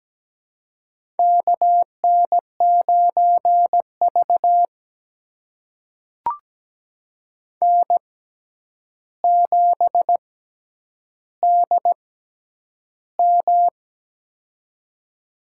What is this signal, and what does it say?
Morse code, heard as a keyed tone, about 17 wpm, spaced as for 9 wpm, 700 Hz. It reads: KN9V N7DM
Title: USA Call Signs - 4 Characters - 6x Character Spacing - 17wpm